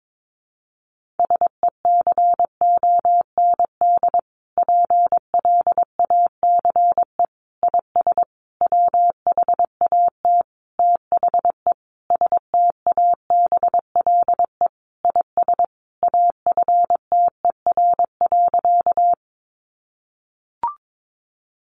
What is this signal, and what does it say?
Morse code, heard as a keyed tone, about 22 wpm, 700 Hz. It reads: SECOND PLACE IS WHAT THE STABLE IS AFTER.
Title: Sentences from Top 1500 Words 22wpm